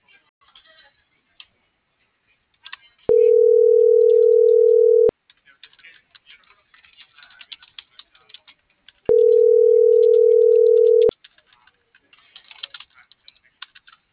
Ringback tone (North America), 440 + 480 Hz.